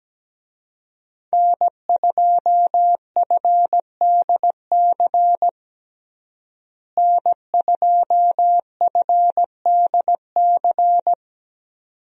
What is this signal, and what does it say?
Morse code, heard as a keyed tone, about 17 wpm, 700 Hz. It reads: N2FDC N2FDC
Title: USA Call Signs - 5 Characters - Repeated 2x - 17wpm